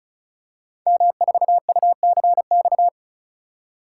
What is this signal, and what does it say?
Morse code, heard as a keyed tone, about 36 wpm, 690 Hz. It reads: M4UCX
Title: Code Groups - 5 Letter:Numbers - 35wpm